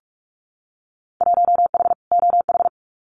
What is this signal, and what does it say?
Morse code, heard as a keyed tone, about 44 wpm, 700 Hz. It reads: 1H OH
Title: Winter Field Day Exchanges 45wpm